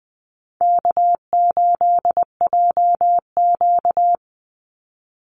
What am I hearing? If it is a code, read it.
Morse code, heard as a keyed tone, about 20 wpm, 700 Hz. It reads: K8JQ
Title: Easy Call Signs Repeated 3x - 20wpm